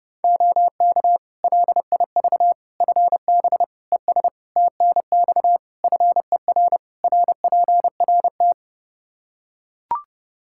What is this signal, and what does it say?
Morse code, heard as a keyed tone, about 32 wpm, 700 Hz. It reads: OK LIV FB ES TNX FER RPRT
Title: Practice QSO - Exchange 2 - Info - 30wpm